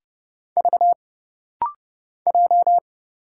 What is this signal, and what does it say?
Morse code, heard as a keyed tone, about 30 wpm, 700 Hz. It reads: V J
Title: Single Letter-Number Alt - Rapid-Fire 30wpm